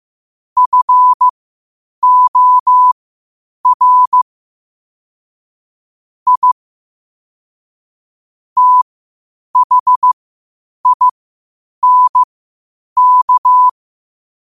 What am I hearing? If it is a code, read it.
Morse code, heard as a keyed tone, about 15 wpm, spaced as for 8 wpm, 1000 Hz. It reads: FOR I THINK
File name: qotdfiles / 8wpm